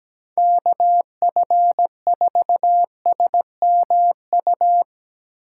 Morse code, heard as a keyed tone, about 18 wpm, 700 Hz. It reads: KF4SMU
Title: Speed Racing - USA 2x3 Call Signs 17wpm